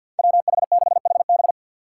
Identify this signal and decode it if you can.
Morse code, heard as a keyed tone, about 50 wpm, 700 Hz. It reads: WH6HB
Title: USA Call Signs - 5 Characters - 2x Character Spacing - 50wpm